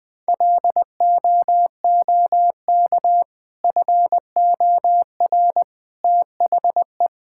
Morse code, heard as a keyed tone, about 20 wpm, 700 Hz. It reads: LOOK FOR THE
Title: Sentences with 2-Word Phrases - 20wpm